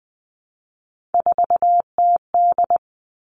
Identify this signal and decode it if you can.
Morse code, heard as a keyed tone, about 20 wpm, 700 Hz. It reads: 4TD
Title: All Cut Numbers - 3-Digit 20wpm